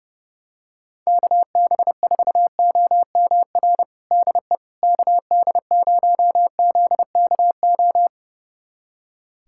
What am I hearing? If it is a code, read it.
Morse code, heard as a keyed tone, about 30 wpm, 700 Hz. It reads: KB4OMR DE KD0ZKO
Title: USA Callsign DE Callsign - 6 Characters - 30wpm